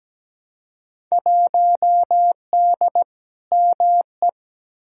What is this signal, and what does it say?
Morse code, heard as a keyed tone, about 17 wpm, 700 Hz. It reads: 1D ME
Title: Field Day Exchanges Encore - 17wpm